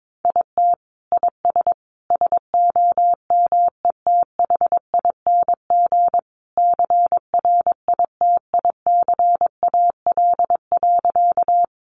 Morse code, heard as a keyed tone, about 22 wpm, 700 Hz. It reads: IT IS SOMETHING CRITICAL.
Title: Sentences from Top 1100 Words 22wpm